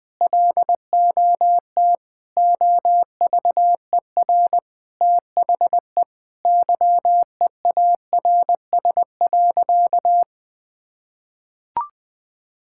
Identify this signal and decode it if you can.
Morse code, heard as a keyed tone, about 21 wpm, 700 Hz. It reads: LOT OVER THE YEARS.
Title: Sentences with 3-Word Phrases - 20wpm